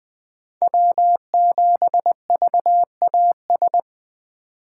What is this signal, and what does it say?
Morse code, heard as a keyed tone, about 21 wpm, 700 Hz. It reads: W7VAS
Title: K1USN SST QSO Practice - 20wpm